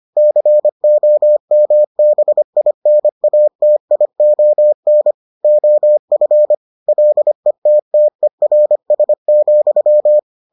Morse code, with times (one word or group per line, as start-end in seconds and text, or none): 0.2-5.1 COMBINATION
5.4-6.5 OF
6.9-10.2 LETTERS,